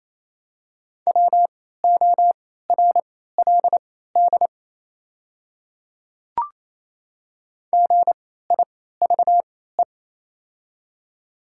Morse code, heard as a keyed tone, about 28 wpm, spaced as for 16 wpm, 700 Hz. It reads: WORLD GIVE
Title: 3x Character Spacing - Top 200 Words - 28wpm